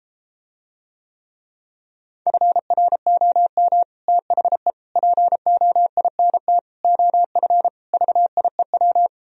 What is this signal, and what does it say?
Morse code, heard as a keyed tone, about 33 wpm, 700 Hz. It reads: FROM THE POINT OF VIEW